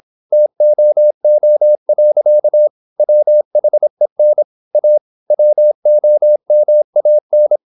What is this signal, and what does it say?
Morse code, heard as a keyed tone, about 26 wpm, 600 Hz. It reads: TOO. WHEN A WOMAN
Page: Never 26wpm | CW med Gnister